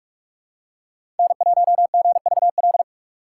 Morse code, heard as a keyed tone, about 46 wpm, 690 Hz. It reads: N1GVL
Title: USA Call Signs - 5 Characters - 6x Character Spacing - 45wpm